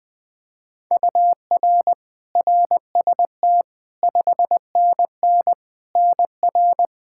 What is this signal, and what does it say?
Morse code, heard as a keyed tone, about 21 wpm, 700 Hz. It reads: UR RST 5NN NR
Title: Practice QSO - Exchange 1 - Beginning - 20wpm